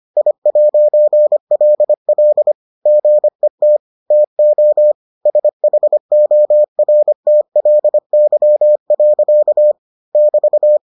Morse code, heard as a keyed tone, about 25 wpm, 600 Hz. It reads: I'LL GET TO SHORTLY. <BT>